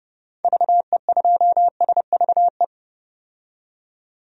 Morse code, heard as a keyed tone, about 32 wpm, 700 Hz. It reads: VE2SVE
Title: DXCC-100 Call Signs - 6 Characters - 30wpm